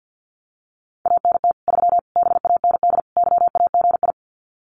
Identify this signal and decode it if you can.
Morse code, heard as a keyed tone, about 50 wpm, 700 Hz. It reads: ANT 3 BAND YAGI
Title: Common QSO Phrases 50wpm